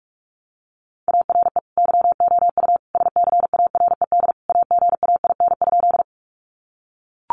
Morse code, heard as a keyed tone, about 44 wpm, 700 Hz. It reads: ARE YOU SCARED AGAIN?
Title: Sentences from Top 700 Words 45wpm